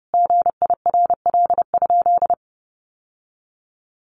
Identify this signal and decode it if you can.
Morse code, heard as a keyed tone, about 28 wpm, 700 Hz. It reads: GIRL?
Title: Sentences from Top 1500 Words 30wpm